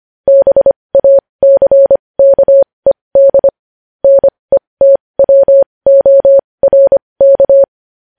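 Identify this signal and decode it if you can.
Morse code, heard as a keyed tone, about 25 wpm, spaced as for 20 wpm, 550 Hz. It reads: BACKED NETWORK
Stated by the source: Morse code news headlines sent at 20.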